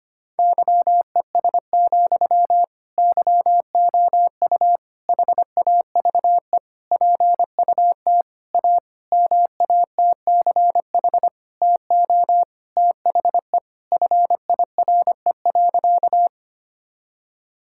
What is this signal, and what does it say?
Morse code, heard as a keyed tone, about 25 wpm, 700 Hz. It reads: YES, YOU HAVE PUT A MATCH TO THE FIRE.
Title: Sentences from Top 1600 Words 25wpm